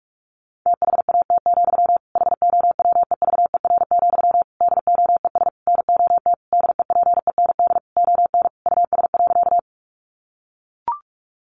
Morse code, heard as a keyed tone, about 44 wpm, 700 Hz. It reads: THAT, HOWEVER, DOES NOT DEPEND ON US.
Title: Sentences from Top 1300 Words 45wpm